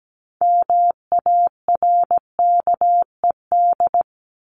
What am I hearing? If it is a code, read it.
Morse code, heard as a keyed tone, about 17 wpm, 700 Hz. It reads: MARKED
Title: ED Suffix - Top 100 Words with Base Words 17wpm